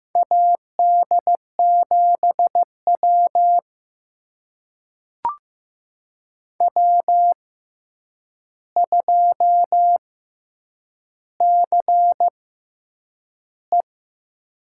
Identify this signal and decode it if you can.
Morse code, heard as a keyed tone, about 15 wpm, spaced as for 8 wpm, 700 Hz. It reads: AD7W W2CE
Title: USA Call Signs - 4 Characters - 6x Character Spacing - 15wpm